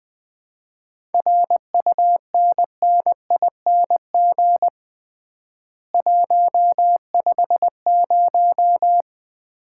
Morse code, heard as a keyed tone, about 20 wpm, 700 Hz. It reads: RUNNING 150
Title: Common QSO Phrases - 3x Word-Spacing - 20wpm